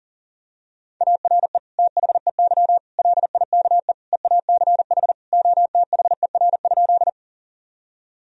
Morse code, heard as a keyed tone, about 40 wpm, 700 Hz. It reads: ARE THEY LIKE EACH OTHER?
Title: Sentences from Top 1000 Words - Review Part 1 - 40wpm